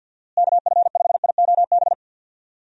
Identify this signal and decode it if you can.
Morse code, heard as a keyed tone, about 50 wpm, 700 Hz. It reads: KF5IOB
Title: USA Call Signs - 6 Characters - Repeated 2x - 50wpm